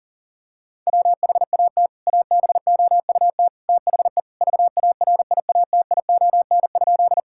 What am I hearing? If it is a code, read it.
Morse code, heard as a keyed tone, about 37 wpm, 700 Hz. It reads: WHAT ABOUT THE VARIATION?
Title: Sentences from Top 1600 Words 40wpm